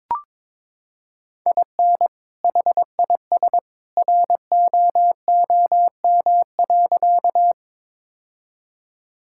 Morse code, heard as a keyed tone, about 21 wpm, 700 Hz. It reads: IN HIS ROOM.
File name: practice / 22wpm / Sentences with 2-Word Phrases - 22wpm